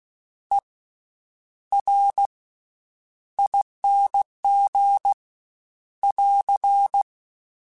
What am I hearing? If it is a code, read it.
Morse code, heard as a keyed tone, about 16 wpm, spaced as for 9 wpm, 800 Hz. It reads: ERING <AR>
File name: Special INGs-mp3s / Special INGs-mp3s / Slow ing @15